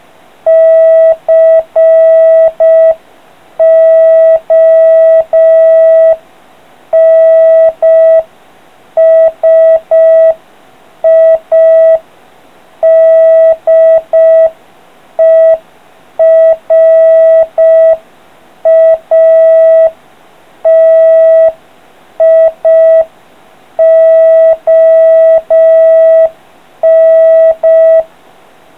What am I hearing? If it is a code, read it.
Morse code, heard as a keyed tone, about 5 wpm, 650 Hz. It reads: CONSIDERATION